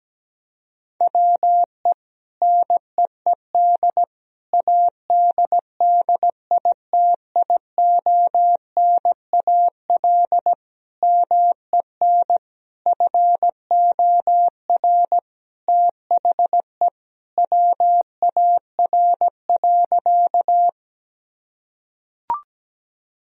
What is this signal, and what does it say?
Morse code, heard as a keyed tone, about 17 wpm, 700 Hz. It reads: WE NEED ADDITIONAL MEN FOR THE WAR.
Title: Sentences from Top 1000 Words - Review Part 4 - 17wpm